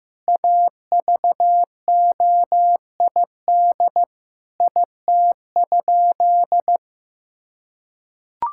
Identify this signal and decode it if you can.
Morse code, heard as a keyed tone, about 15 wpm, 700 Hz. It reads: AVOID IT?
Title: Sentences from Top 1000 Words - Review Part 6 - 15wpm